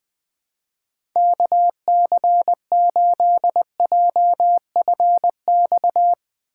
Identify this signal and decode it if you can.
Morse code, heard as a keyed tone, about 21 wpm, 700 Hz. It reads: KC8JFX
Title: USA Call Signs - 6 Characters - 4x Character Spacing - 20wpm